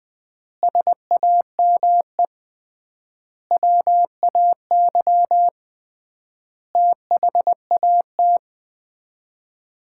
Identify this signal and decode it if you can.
Morse code, heard as a keyed tone, about 20 wpm, 700 Hz. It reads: SAME WAY THAT
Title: 3x Word-Spacing - Sets of 5 Words - 20wpm